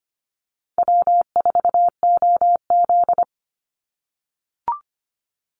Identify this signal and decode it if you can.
Morse code, heard as a keyed tone, about 25 wpm, 700 Hz. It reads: W4OZ
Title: USA Call Signs 25wpm